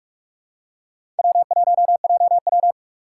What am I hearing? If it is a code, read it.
Morse code, heard as a keyed tone, about 46 wpm, 700 Hz. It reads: W1JW
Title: USA Call Signs - 4 Characters - 2x Character Spacing - 45wpm